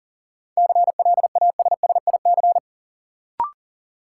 Morse code, heard as a keyed tone, about 40 wpm, 700 Hz. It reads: CLASSIC
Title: Sentences from Top 1000 Words - New Word Drill - 40wpm